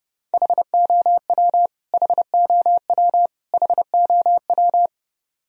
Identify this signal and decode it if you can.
Morse code, heard as a keyed tone, about 32 wpm, 700 Hz. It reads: HOW HOW HOW
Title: Top 100 Words Repeated 30wpm